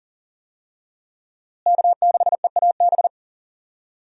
Morse code, heard as a keyed tone, about 39 wpm, 700 Hz. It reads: K6EAB
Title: USA Call Signs - 5 Characters - Repeated 2x - 40wpm